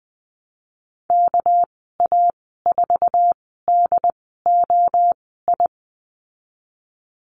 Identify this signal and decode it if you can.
Morse code, heard as a keyed tone, about 20 wpm, spaced as for 14 wpm, 700 Hz. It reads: KA4DOI
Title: 245 Lesson 39g - Letter z - Call Signs 20wpm